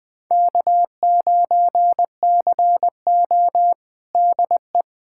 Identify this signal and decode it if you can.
Morse code, heard as a keyed tone, about 21 wpm, 700 Hz. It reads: K9CO DE
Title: USA Callsign DE Callsign - 4 Characters - 20wpm